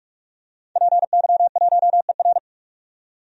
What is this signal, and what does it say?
Morse code, heard as a keyed tone, about 46 wpm, 700 Hz. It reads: PY1ER
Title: DXCC-100 Call Signs - 5 Characters - 45wpm